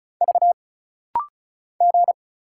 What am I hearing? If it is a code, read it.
Morse code, heard as a keyed tone, about 36 wpm, 700 Hz. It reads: V G
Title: Single Letter-Number Alt - Rapid-Fire 35wpm